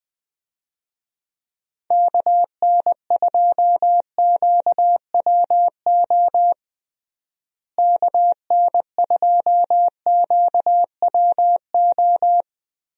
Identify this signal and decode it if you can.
Morse code, heard as a keyed tone, about 20 wpm, 700 Hz. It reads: KN2QWO KN2QWO
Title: USA Call Signs - 6 Characters - Repeated 2x - 20wpm